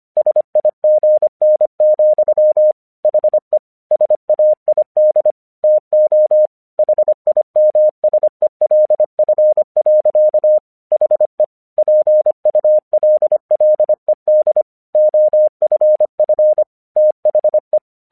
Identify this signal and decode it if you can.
Morse code, heard as a keyed tone, about 25 wpm, 610 Hz. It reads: SIGN, HE SAID TO HIMSELF. HE PULLED OFF THE
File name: 1 English 25wpm / AQC28